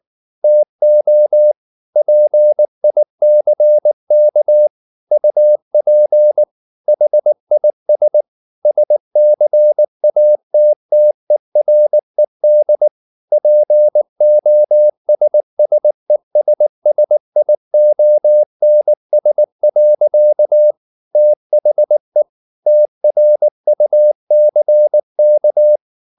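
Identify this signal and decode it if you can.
Morse code, heard as a keyed tone, about 19 wpm, 600 Hz. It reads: TO PICK UP HIS SCATTERED POSSESSIONS. THE TRUCK
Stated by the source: Morse code at 19 wpm from CW med Gnister, reading Ken Follett's novel Never in English.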